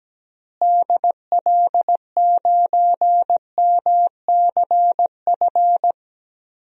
Morse code, heard as a keyed tone, about 17 wpm, 700 Hz. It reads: DL9MCF
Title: DXCC-100 Call Signs - 6 Characters - Repeated 3x - 17wpm